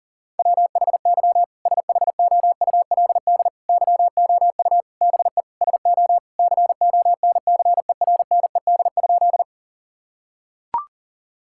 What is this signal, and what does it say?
Morse code, heard as a keyed tone, about 40 wpm, 700 Hz. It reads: WHY SHOULD YOU BE SO CONCERNED?
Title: Sentences from Top 1000 Words - Review Part 6 - 40wpm